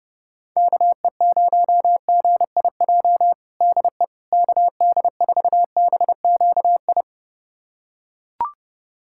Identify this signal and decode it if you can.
Morse code, heard as a keyed tone, about 30 wpm, 700 Hz. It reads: KE0GIJ DE KD4BQI
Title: USA Callsign DE Callsign - 6 Characters - 30wpm